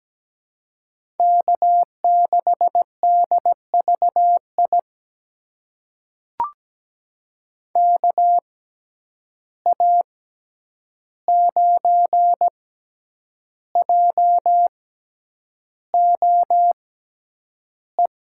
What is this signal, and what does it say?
Morse code, heard as a keyed tone, about 17 wpm, spaced as for 8 wpm, 700 Hz. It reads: K6DVI KA9JOE
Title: USA Call Signs - 6x Character Spacing - 17wpm